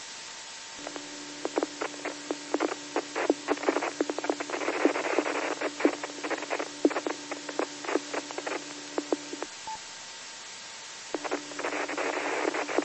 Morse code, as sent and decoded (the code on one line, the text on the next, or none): none